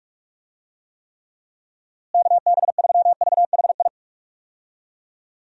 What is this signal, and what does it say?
Morse code, heard as a keyed tone, about 46 wpm, 700 Hz. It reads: KB3VHI
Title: USA Call Signs - 6 Characters - Repeated 2x - 45wpm